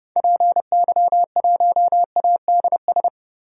Morse code, heard as a keyed tone, about 30 wpm, 700 Hz. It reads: PY1ADS
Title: DXCC-100 Call Signs - 6 Characters - 30wpm